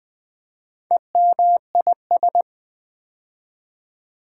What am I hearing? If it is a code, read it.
Morse code, heard as a keyed tone, about 20 wpm, 700 Hz. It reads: EMIS
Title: Build A Word - 2 Letters At A Time - 8-Letter Words - 20wpm